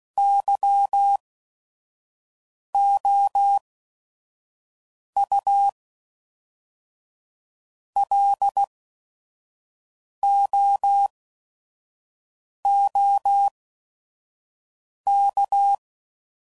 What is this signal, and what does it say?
Morse code, heard as a keyed tone, about 16 wpm, spaced as for 5 wpm, 800 Hz. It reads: YOU LOOK
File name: CW Code Groups @5WPM